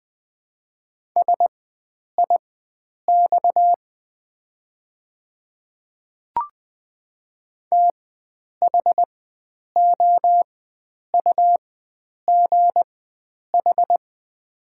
Morse code, heard as a keyed tone, about 20 wpm, spaced as for 9 wpm, 700 Hz. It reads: SIX THOUGH
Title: 4x Character Spacing - Top 500 Words - 20wpm